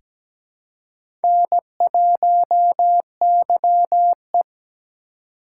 Morse code, heard as a keyed tone, about 17 wpm, 700 Hz. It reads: N1YE
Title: USA Call Signs - 4 Characters - 4x Character Spacing - 17wpm